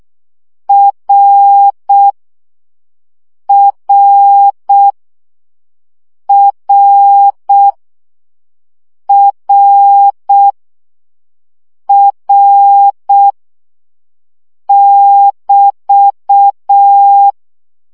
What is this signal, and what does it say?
Morse code, heard as a keyed tone, about 6 wpm, 800 Hz. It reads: R R R R R <BT>
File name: CW gyakorlas r 5 baud